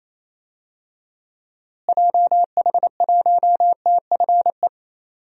Morse code, heard as a keyed tone, about 28 wpm, 700 Hz. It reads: JH1TFE
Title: DXCC-100 Call Signs - 6 Characters - Repeated 2x - 28wpm